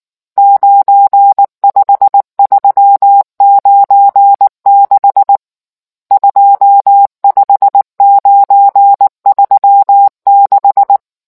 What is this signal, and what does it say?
Morse code, heard as a keyed tone, about 19 wpm, 800 Hz. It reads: 95396 25936